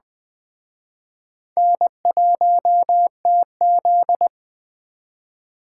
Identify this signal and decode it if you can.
Morse code, heard as a keyed tone, about 21 wpm, 700 Hz. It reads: N1TZ
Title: Speed Racing - USA 1x2 Call Signs 17wpm